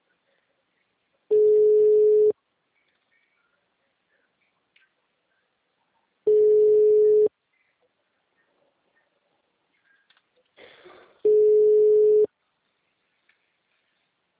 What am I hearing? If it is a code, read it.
Ringback tone (Europe, 425 Hz).